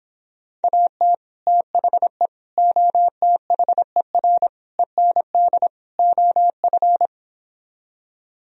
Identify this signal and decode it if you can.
Morse code, heard as a keyed tone, about 26 wpm, 700 Hz. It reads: AT THE OTHER END OF